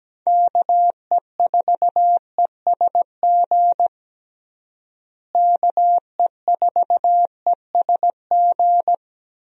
Morse code, heard as a keyed tone, about 17 wpm, 700 Hz. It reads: KE4ESG KE4ESG
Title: USA Call Signs - 6 Characters - Repeated 3x - 17wpm